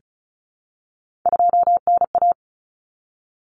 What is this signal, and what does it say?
Morse code, heard as a keyed tone, about 34 wpm, 700 Hz. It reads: 2NA
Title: Common Cut Numbers - 3-Digit 35wpm